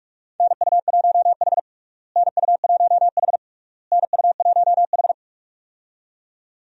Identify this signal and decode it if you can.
Morse code, heard as a keyed tone, about 48 wpm, 700 Hz. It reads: NU1H NU1H NU1H
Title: USA Call Signs - 4 Characters - Repeated 3x - 45wpm